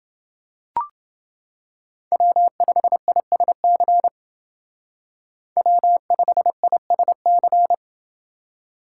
Morse code, heard as a keyed tone, about 29 wpm, 700 Hz. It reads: W5ISC W5ISC